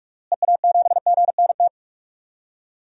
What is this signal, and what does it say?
Morse code, heard as a keyed tone, about 43 wpm, 700 Hz. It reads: EA7GNT
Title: DXCC-100 Call Signs - 6 Characters - Repeated 3x - 45wpm